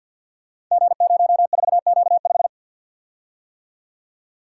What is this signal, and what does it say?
Morse code, heard as a keyed tone, about 49 wpm, 700 Hz. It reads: G04Q5
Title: Code Groups - 5 Letter:Numbers - 50wpm